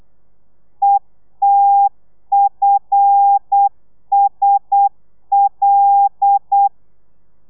Morse code, heard as a keyed tone, about 8 wpm, 800 Hz. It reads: ETFSL
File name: telegrafia / i2tfj / Lez5 35